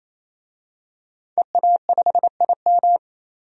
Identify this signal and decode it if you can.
Morse code, heard as a keyed tone, about 28 wpm, 700 Hz. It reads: EA5IM
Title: DXCC-100 Call Signs - 5 Characters - 28wpm